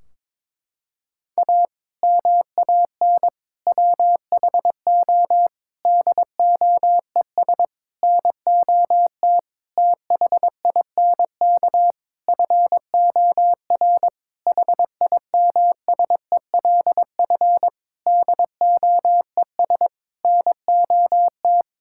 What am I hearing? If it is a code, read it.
Morse code, heard as a keyed tone, about 22 wpm, 700 Hz. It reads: A MAN WHO DOES NOT THINK FOR HIMSELF DOES NOT
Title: Sentences from Top 1600 Words 22wpm